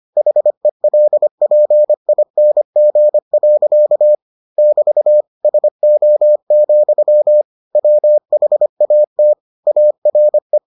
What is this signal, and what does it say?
Morse code, heard as a keyed tone, about 25 wpm, 600 Hz. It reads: HELPING. <BT> SO, WHAT ARE